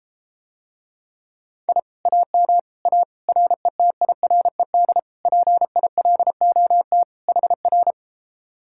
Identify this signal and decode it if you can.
Morse code, heard as a keyed tone, about 33 wpm, 700 Hz. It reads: I AM A RETIRED PILOT HR